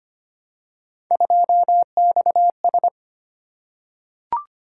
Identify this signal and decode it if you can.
Morse code, heard as a keyed tone, about 25 wpm, 700 Hz. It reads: 2XS
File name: Three Letter-Number Combinations - 25wpm